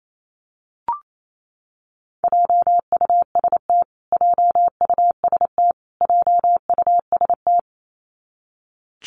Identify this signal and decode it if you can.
Morse code, heard as a keyed tone, about 28 wpm, 700 Hz. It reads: JUST JUST JUST
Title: Top 300 Words Repeated 28wpm